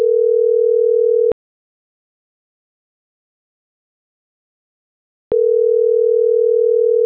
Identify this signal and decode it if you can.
Ringback tone (North America), 440 + 480 Hz.